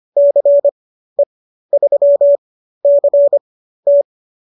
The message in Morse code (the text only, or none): CE3CT